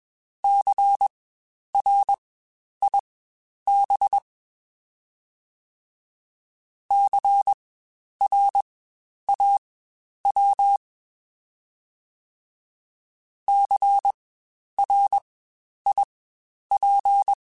Morse code, heard as a keyed tone, about 21 wpm, spaced as for 10 wpm, 800 Hz. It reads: CRIB CRAW CRIP
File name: Spaced Alphas / Spaced Alphas mp3s / Cs sp @20